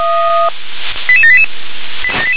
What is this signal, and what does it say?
Five-tone selective call (ZVEI), heard as quick runs of five tones, tones of about 70 ms. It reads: 88788